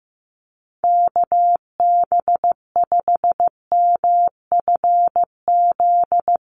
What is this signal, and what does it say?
Morse code, heard as a keyed tone, about 15 wpm, 700 Hz. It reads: KB5MFZ
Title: USA Call Signs 15wpm